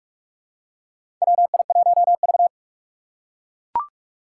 Morse code, heard as a keyed tone, about 44 wpm, 700 Hz. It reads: WI1V
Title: USA Call Signs - 6x Character Spacing - 45wpm